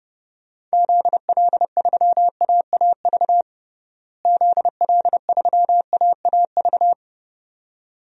Morse code, heard as a keyed tone, about 31 wpm, 700 Hz. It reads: ZL3AAV ZL3AAV
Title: DXCC-100 Call Signs - 6 Characters - Repeated 2x - 30wpm